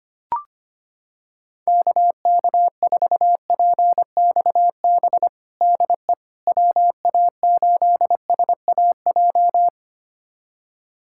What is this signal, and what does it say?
Morse code, heard as a keyed tone, about 25 wpm, 700 Hz. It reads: KK4PXB DE WA8SAJ
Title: USA Callsign DE Callsign - 6 Characters - 25wpm